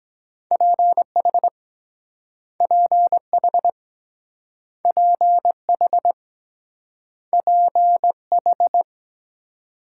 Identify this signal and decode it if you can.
Morse code, heard as a keyed tone, about 23 wpm, 700 Hz. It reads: PH PH PH PH